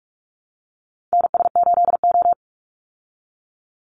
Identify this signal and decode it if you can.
Morse code, heard as a keyed tone, about 44 wpm, 700 Hz. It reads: NS8O
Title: Call Signs Repeated 2x - 45wpm